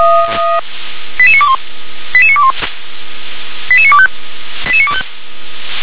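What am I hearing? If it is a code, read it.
Five-tone selective call (ZVEI), heard as quick runs of five tones, tones of about 70 ms. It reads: 88921 88921 88925 88925
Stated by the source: ZVEI five-tone selective call 88955, received on SDR.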